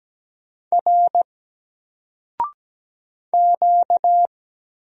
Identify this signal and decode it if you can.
Morse code, heard as a keyed tone, about 17 wpm, 700 Hz. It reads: R Q
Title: Single Character (Letter:Number:Punctuation:Prosign) Alt 17wpm